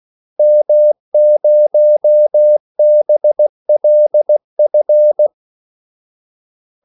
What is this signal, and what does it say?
Morse code, heard as a keyed tone, about 16 wpm, 600 Hz. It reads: M0BLF
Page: Kaldesignaler 16 wpm